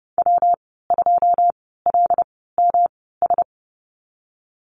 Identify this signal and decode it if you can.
Morse code, heard as a keyed tone, about 30 wpm, spaced as for 17 wpm, 700 Hz. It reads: W2LMS
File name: Lesson 19g - Letter w - Call Signs 30wpm